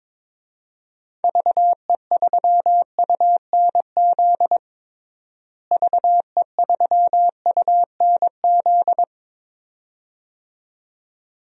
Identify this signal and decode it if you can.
Morse code, heard as a keyed tone, about 22 wpm, 700 Hz. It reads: VE3UNZ VE3UNZ
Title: DXCC-100 Call Signs - 6 Characters - Repeated 2x - 22wpm